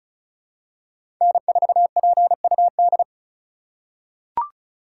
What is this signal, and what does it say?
Morse code, heard as a keyed tone, about 34 wpm, 700 Hz. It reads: N4PUD
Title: USA Call Signs - 5 Characters - 4x Character Spacing - 35wpm